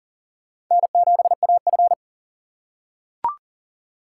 Morse code, heard as a keyed tone, about 40 wpm, 700 Hz. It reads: N7AF
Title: USA Call Signs - 4 Characters - 3x Character Spacing - 40wpm